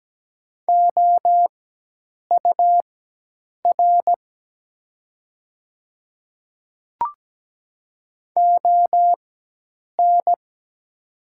Morse code, heard as a keyed tone, about 17 wpm, spaced as for 8 wpm, 700 Hz. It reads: OUR ON
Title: 4x Character Spacing - Top 200 Words - 17wpm